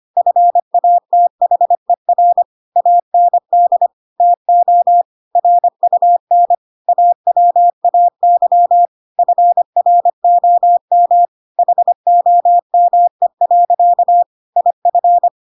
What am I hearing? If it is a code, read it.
Morse code, heard as a keyed tone, about 25 wpm, 700 Hz. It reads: FATHER AND TO RUN AWAY FROM HOME. IF
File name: Pinocchio / Pinocchio ch5 25 25 ik1hge